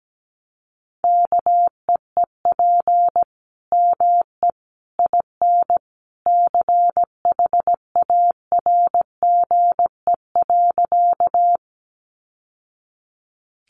Morse code, heard as a keyed tone, about 17 wpm, 700 Hz. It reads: KEEP ME IN CHARGE.
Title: Sentences from Top 900 Words 17wpm